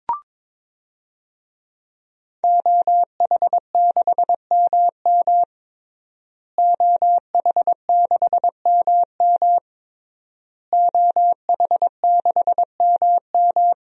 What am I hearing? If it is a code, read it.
Morse code, heard as a keyed tone, about 22 wpm, 700 Hz. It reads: OH6MM OH6MM OH6MM
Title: DXCC-100 Call Signs - 5 Characters - Repeated 3x - 22wpm